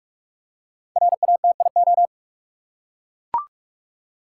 Morse code, heard as a keyed tone, about 43 wpm, 700 Hz. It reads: RATIO
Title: Five Letter Words - 45wpm